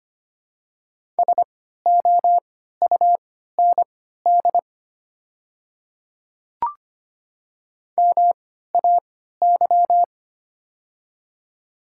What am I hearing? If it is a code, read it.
Morse code, heard as a keyed tone, about 25 wpm, spaced as for 14 wpm, 700 Hz. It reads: SOUND MAY
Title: 3x Character Spacing - Top 100 Words - 25wpm